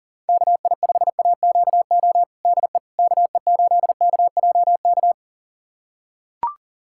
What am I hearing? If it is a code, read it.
Morse code, heard as a keyed tone, about 40 wpm, 700 Hz. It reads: KI5AQO DE KE8KJK
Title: USA Callsign DE Callsign 40wpm